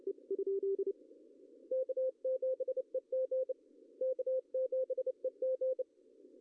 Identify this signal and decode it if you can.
Morse code, heard as a keyed tone, about 25 wpm, 520 Hz. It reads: K7EG K7EG
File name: CW / cqww-cw-k5tr